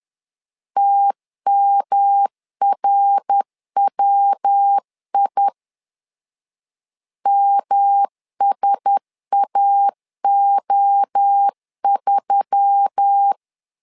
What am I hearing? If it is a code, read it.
Morse code, heard as a keyed tone, about 10 wpm, 800 Hz. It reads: TMRWI MSAO3
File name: LEZIONI / 4 Q1S3 10wpm 2427